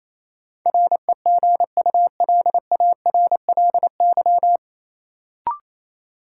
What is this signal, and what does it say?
Morse code, heard as a keyed tone, about 28 wpm, 700 Hz. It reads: REGULARLY
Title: Sentences from Top 1400 Words - New Word Drill - 28wpm